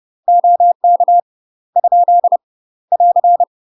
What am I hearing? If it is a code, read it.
Morse code, heard as a keyed tone, about 30 wpm, 700 Hz. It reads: OK ? <AR>
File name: UK MASTER QSOs TRAINING / 30WPM / UK MASTER QSOs 0015